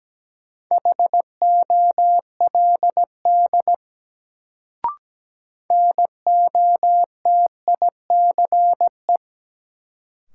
Morse code, heard as a keyed tone, about 17 wpm, 700 Hz. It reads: HOLD NOTICE
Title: Sentences from Top 400 Words - New Word Drill - 17wpm